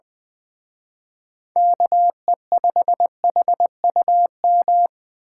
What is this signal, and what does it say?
Morse code, heard as a keyed tone, about 24 wpm, 700 Hz. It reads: KE5HUM
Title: Speed Racing - USA 2x3 Call Signs 20wpm